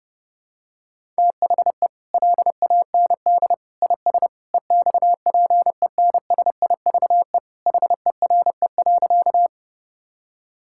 Morse code, heard as a keyed tone, about 30 wpm, 700 Hz. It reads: THE LAND IS EXPENSIVE HERE.
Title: Sentences from Top 1000 Words - Review Part 6 - 30wpm